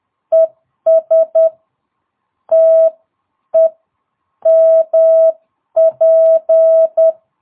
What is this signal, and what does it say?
Morse code, heard as a keyed tone, about 10 wpm, spaced as for 8 wpm, 650 Hz. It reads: ES TEMP
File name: jh8cbh / 2022cw